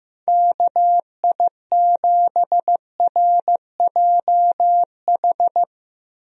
Morse code, heard as a keyed tone, about 15 wpm, 700 Hz. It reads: KI7RJH
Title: USA Call Signs - 6 Characters - Repeated 3x - 15wpm